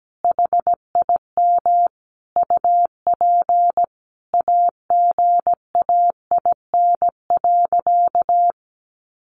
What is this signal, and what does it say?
Morse code, heard as a keyed tone, about 17 wpm, 700 Hz. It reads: HIM UP AGAIN.
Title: Sentences from Top 1400 Words 17wpm